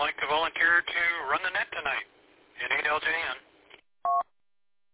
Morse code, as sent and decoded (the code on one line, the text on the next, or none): none